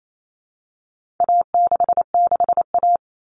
Morse code, heard as a keyed tone, about 28 wpm, 700 Hz. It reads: A66A
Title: All Cut Numbers - 4-Digit 28wpm